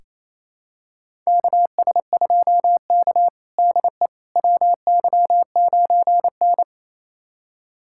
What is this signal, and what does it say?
Morse code, heard as a keyed tone, about 28 wpm, 700 Hz. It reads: KS2K DE WY9N
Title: USA Callsign DE Callsign - 4 Characters - 28wpm